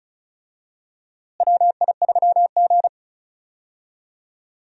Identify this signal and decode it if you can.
Morse code, heard as a keyed tone, about 34 wpm, 700 Hz. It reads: WI3G
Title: USA Call Signs - 4 Characters - Repeated 3x - 35wpm